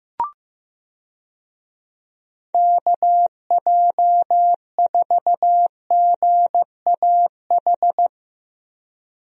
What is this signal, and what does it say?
Morse code, heard as a keyed tone, about 15 wpm, 700 Hz. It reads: KJ4GAH
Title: USA Call Signs - 6 Characters - Repeated 3x - 15wpm